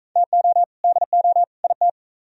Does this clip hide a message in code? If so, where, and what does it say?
Yes: Morse code from 0.2 to 1.9 s, heard as a keyed tone, about 44 wpm, 700 Hz. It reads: TO DO IT